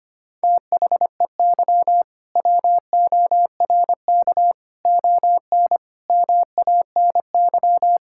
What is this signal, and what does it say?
Morse code, heard as a keyed tone, about 25 wpm, 700 Hz. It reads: THEY WORK ON MANY
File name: practice / 25wpm / All-Age Appropriate Jokes - 25wpm